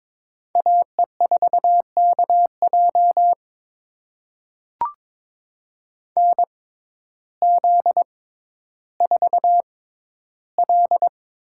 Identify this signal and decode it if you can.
Morse code, heard as a keyed tone, about 22 wpm, spaced as for 12 wpm, 700 Hz. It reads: AE4KJ NZ4L
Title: USA Call Signs - 6x Character Spacing - 22wpm